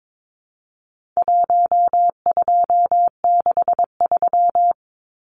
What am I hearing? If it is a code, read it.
Morse code, heard as a keyed tone, about 22 wpm, 700 Hz. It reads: 1263
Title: Numbers - 4-Digit 22wpm